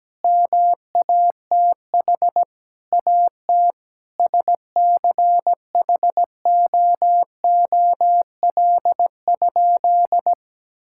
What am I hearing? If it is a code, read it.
Morse code, heard as a keyed tone, about 17 wpm, 700 Hz. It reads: MATH AT SCHOOL?
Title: Sentences from Top 1000 Words - Review Part 6 - 17wpm